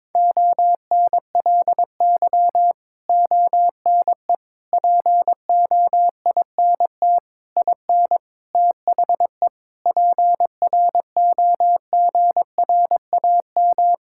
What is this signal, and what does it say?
Morse code, heard as a keyed tone, about 22 wpm, 700 Hz. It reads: ONLY ONE POINT IN THE PROGRAM
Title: Sentences from Top 1000 Words - Review Part 5 - 22wpm